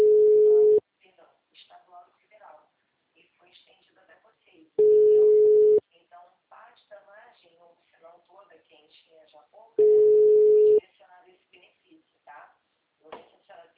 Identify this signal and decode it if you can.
Ringback tone (Europe, 425 Hz), with speech.
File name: out- -2041